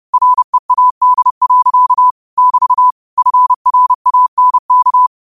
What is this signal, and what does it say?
Morse code, heard as a keyed tone, about 30 wpm, 1000 Hz. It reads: READ. <BT> FRANK
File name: qotdfiles / 30wpm